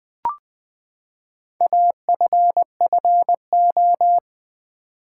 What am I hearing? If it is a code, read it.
Morse code, heard as a keyed tone, about 20 wpm, 700 Hz. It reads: AFFO
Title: Build A Word - 4 Letters At A Time - 10-Letter Words - 20wpm